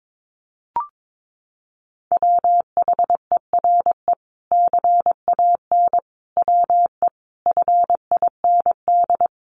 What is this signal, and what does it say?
Morse code, heard as a keyed tone, about 22 wpm, 700 Hz. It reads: WHERE CAN WE FIND
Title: Sentences from Top 1500 Words 22wpm